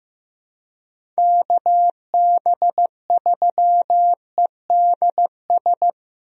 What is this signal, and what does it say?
Morse code, heard as a keyed tone, about 15 wpm, 700 Hz. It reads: KB3EDS
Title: USA Call Signs - 6 Characters - Repeated 3x - 15wpm